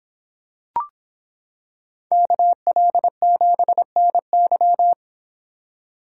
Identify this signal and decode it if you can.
Morse code, heard as a keyed tone, about 26 wpm, 700 Hz. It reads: KL7NY